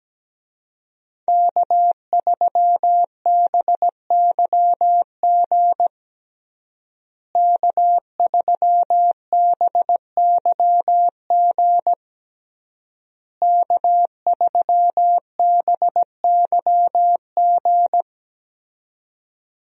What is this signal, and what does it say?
Morse code, heard as a keyed tone, about 17 wpm, 700 Hz. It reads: K3BYG K3BYG K3BYG
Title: USA Call Signs - 5 Characters - Repeated 3x - 17wpm